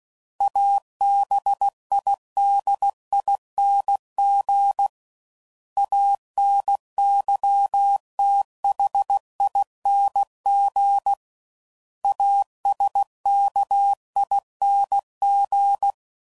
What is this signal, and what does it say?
Morse code, heard as a keyed tone, about 16 wpm, 800 Hz. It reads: ABIDING ANYTHING ASKING
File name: Suffixes / Special INGs / Sngl Ings